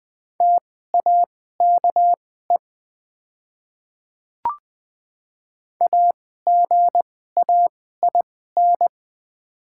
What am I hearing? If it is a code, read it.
Morse code, heard as a keyed tone, about 20 wpm, spaced as for 14 wpm, 700 Hz. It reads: TAKE AGAIN
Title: 2x Character Spacing - Top 300 Words - 20wpm